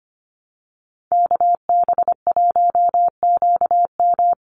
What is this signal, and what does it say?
Morse code, heard as a keyed tone, about 25 wpm, 690 Hz. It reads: KB1QM
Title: USA Call Signs 25wpm